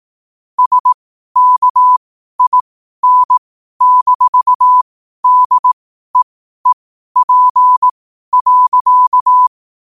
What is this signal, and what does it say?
Morse code, heard as a keyed tone, about 18 wpm, spaced as for 12 wpm, 1000 Hz. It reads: SKIN-DEEP.
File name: qotdfiles / 12wpm